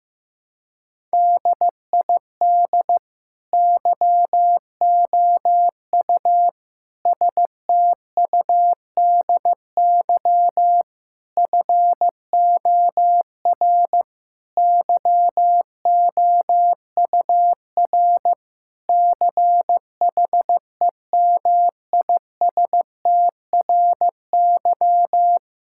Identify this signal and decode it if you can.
Morse code, heard as a keyed tone, about 15 wpm, 700 Hz. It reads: DID YOU STUDY FOR YOUR CHEMISTRY
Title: Sentences from Top 1000 Words - Review Part 6 - 15wpm